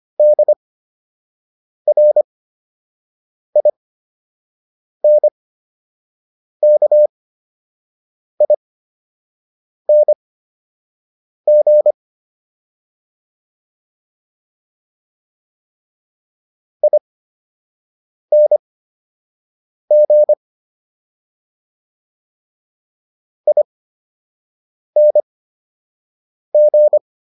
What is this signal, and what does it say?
Morse code, heard as a keyed tone, about 25 wpm, spaced as for 6 wpm, 600 Hz. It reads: DRINKING ING ING